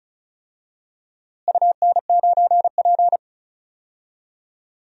Morse code, heard as a keyed tone, about 34 wpm, 700 Hz. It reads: UN9P
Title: DXCC-100 Call Signs - 4 Characters - Repeated 2x - 35wpm